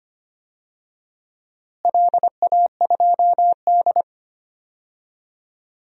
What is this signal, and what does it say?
Morse code, heard as a keyed tone, about 25 wpm, 700 Hz. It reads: LA2D
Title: DXCC-100 Call Signs - 4 Characters - Repeated 3x - 25wpm